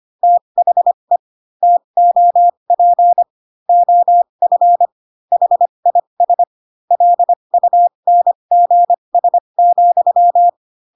Morse code, heard as a keyed tone, about 25 wpm, spaced as for 22 wpm, 700 Hz. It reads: THE TOP OF HIS LUNGS,